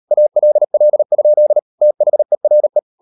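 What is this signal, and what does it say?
Morse code, heard as a keyed tone, about 38 wpm, 600 Hz. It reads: ALL? THERE
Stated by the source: Morse code practice at 38 WPM, sending Ken Follett's novel Never in English, from CW med Gnister.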